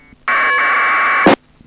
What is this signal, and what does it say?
Packet radio (AX.25, as in APRS).
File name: uploader / iss aprs